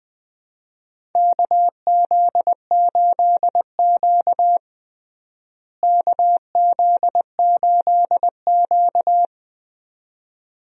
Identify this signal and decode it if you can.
Morse code, heard as a keyed tone, about 20 wpm, 700 Hz. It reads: KZ8Q KZ8Q
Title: USA Call Signs - 4 Characters - Repeated 2x - 20wpm